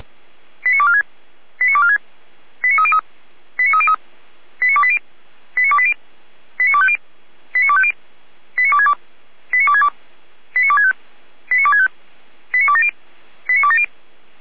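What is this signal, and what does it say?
Five-tone selective call (ZVEI), heard as quick runs of five tones, tones of about 70 ms. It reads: 89246 89246 89303 89303 89280 89280 89250 89250 89262 89262 89265 89265 89270 89270
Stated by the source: ZVEI five-tone selective call received on an SDR, from the Feuerwehr Orlamünde audio server.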